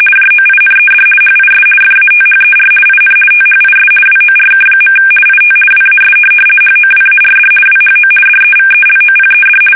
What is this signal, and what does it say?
RTTY, about 50 baud. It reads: JOINED HIS FATHER IN THIS WORK. IT WAS HOWARD WHO DEVELOPED
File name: fsk850-50B / teletype-corp